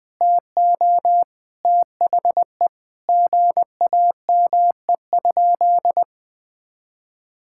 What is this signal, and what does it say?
Morse code, heard as a keyed tone, about 20 wpm, 700 Hz. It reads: TO THE GAME?
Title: Sentences from Top 1600 Words 20wpm